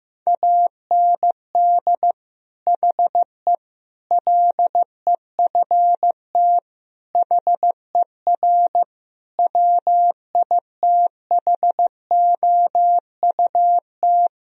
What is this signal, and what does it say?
Morse code, heard as a keyed tone, about 15 wpm, 700 Hz. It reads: AND HE LEFT HER WITHOUT
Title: Sentences from Top 1000 Words - Review Part 3 - 15wpm